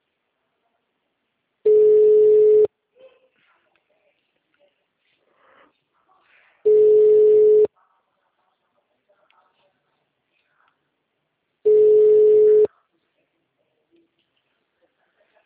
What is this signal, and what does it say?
Ringback tone (Europe, 425 Hz), with speech.